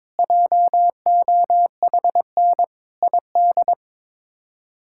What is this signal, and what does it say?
Morse code, heard as a keyed tone, about 21 wpm, 700 Hz. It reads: JOHN ID
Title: K1USN SST QSO Practice - 22wpm